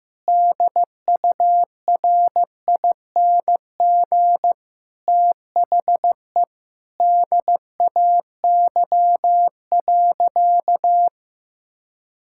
Morse code, heard as a keyed tone, about 15 wpm, 700 Hz. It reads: DURING THE DAY.
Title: Sentences from Top 1000 Words - Review Part 5 - 15wpm